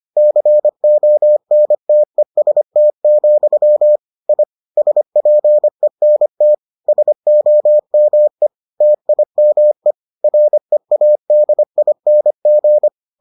Morse code, heard as a keyed tone, about 25 wpm, 600 Hz. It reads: CONTEST, I SPENT SOME TIME READING